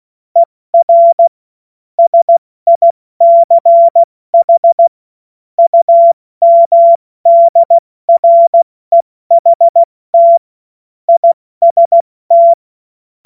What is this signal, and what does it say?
Morse code, heard as a keyed tone, about 16 wpm, 680 Hz. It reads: ER SICH UMDREHT IST
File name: Morsekurs / Witze / Witz 01. August 2023 70 F80